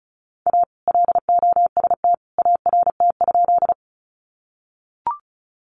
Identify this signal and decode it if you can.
Morse code, heard as a keyed tone, about 34 wpm, 700 Hz. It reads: A LOST ART?
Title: Sentences from Top 1400 Words 35wpm